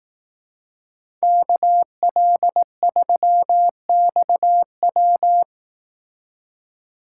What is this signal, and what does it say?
Morse code, heard as a keyed tone, about 18 wpm, 700 Hz. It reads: KL3XW
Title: Speed Racing - USA Call Signs 15wpm